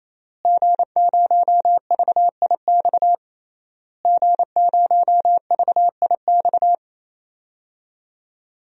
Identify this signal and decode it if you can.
Morse code, heard as a keyed tone, about 28 wpm, 700 Hz. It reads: G0VIX G0VIX
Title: DXCC-100 Call Signs - 5 Characters - Repeated 3x - 28wpm